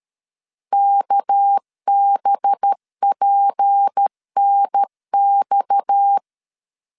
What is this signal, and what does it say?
Morse code, heard as a keyed tone, about 13 wpm, 800 Hz. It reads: KBPNX
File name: CW / DET / 1 6 1500C S A 12wpm 3440